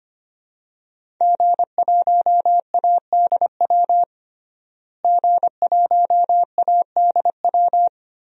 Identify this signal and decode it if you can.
Morse code, heard as a keyed tone, about 25 wpm, 700 Hz. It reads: G1ADW G1ADW
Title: DXCC-100 Call Signs - 5 Characters - Repeated 2x - 25wpm